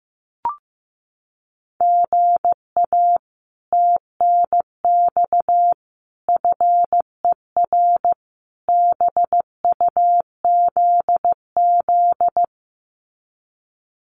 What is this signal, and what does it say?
Morse code, heard as a keyed tone, about 15 wpm, 700 Hz. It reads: GA TNX FER BUZZ
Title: Common QSO Phrases Encore 15wpm